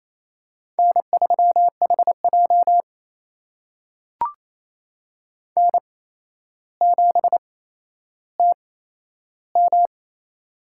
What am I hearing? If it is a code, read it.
Morse code, heard as a keyed tone, about 28 wpm, spaced as for 12 wpm, 700 Hz. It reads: N3HJ N7TM
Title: USA Call Signs - 4 Characters - 8x Character Spacing - 28wpm